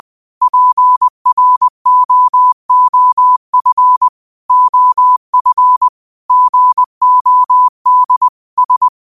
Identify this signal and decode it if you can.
Morse code, heard as a keyed tone, about 20 wpm, 1000 Hz. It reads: PROOF OF GOD S